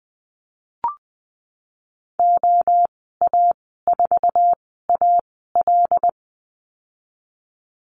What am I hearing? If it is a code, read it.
Morse code, heard as a keyed tone, about 20 wpm, spaced as for 15 wpm, 700 Hz. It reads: OA4AL
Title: 179 Lesson 29g - Slash - Call Signs 20wpm